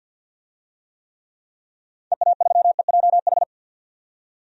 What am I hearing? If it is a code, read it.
Morse code, heard as a keyed tone, about 51 wpm, 700 Hz. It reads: EA3EJH
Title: DXCC-100 Call Signs - 6 Characters - 50wpm